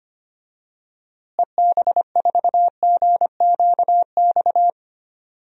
Morse code, heard as a keyed tone, about 25 wpm, 700 Hz. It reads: EB4GQX
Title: DXCC-100 Call Signs - 6 Characters - Repeated 2x - 25wpm